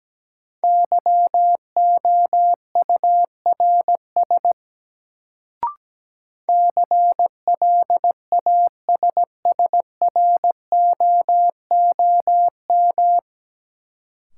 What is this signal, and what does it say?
Morse code, heard as a keyed tone, about 17 wpm, 700 Hz. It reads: YOURS CLASSROOM
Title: Sentences from Top 1500 Words - New Word Drill Alt - 17wpm